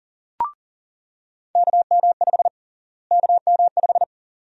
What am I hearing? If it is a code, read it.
Morse code, heard as a keyed tone, about 40 wpm, 700 Hz. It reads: KM5 KM5
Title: USA Call Sign Prefixes - Repeated 2x - 40wpm